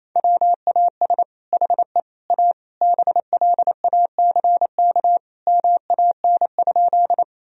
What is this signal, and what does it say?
Morse code, heard as a keyed tone, about 28 wpm, 700 Hz. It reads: WAS HE A BLACK MAN?
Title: Sentences from Top 1000 Words - Review Part 2 - 28wpm